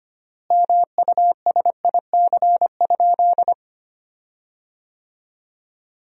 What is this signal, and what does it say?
Morse code, heard as a keyed tone, about 25 wpm, 700 Hz. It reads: MUSIC?
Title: 3x Word-Spacing - Sentences from Top 300 Words - 25wpm